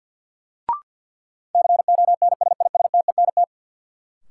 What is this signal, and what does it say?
Morse code, heard as a keyed tone, about 49 wpm, 700 Hz. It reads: CONSISTENT
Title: Sentences from Top 1500 Words - New Word Drill Alt - 50wpm